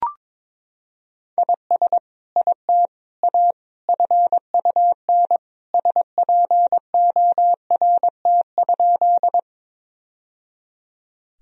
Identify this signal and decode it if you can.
Morse code, heard as a keyed tone, about 21 wpm, 700 Hz. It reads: IS IT A FUN SPORT?
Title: Sentences from Top 1000 Words - Review Part 4 - 22wpm